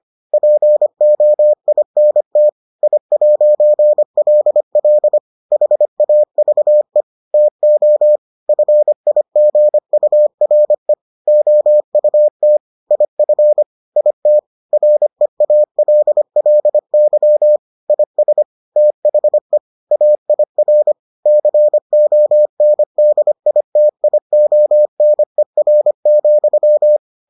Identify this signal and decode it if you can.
Morse code, heard as a keyed tone, about 25 wpm, 600 Hz. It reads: POINT I'LL HAVE TO FIGURE OUT IF IT REALLY IS THE AIR CONDITIONER,